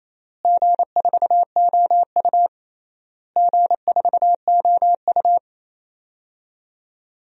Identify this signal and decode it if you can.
Morse code, heard as a keyed tone, about 27 wpm, 700 Hz. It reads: G4OU G4OU
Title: DXCC-100 Call Signs - 4 Characters - Repeated 3x - 28wpm